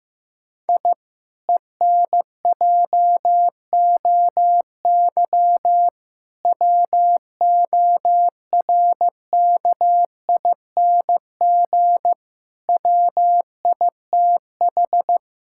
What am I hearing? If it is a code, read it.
Morse code, heard as a keyed tone, about 16 wpm, 700 Hz. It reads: I ENJOY WORKING WITH
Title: Sentences from Top 1000 Words - Review Part 5 - 15wpm